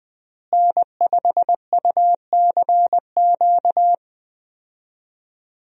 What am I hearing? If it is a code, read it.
Morse code, heard as a keyed tone, about 20 wpm, 700 Hz. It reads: N5UCQ
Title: USA Call Signs - 5 Characters - Repeated 3x - 20wpm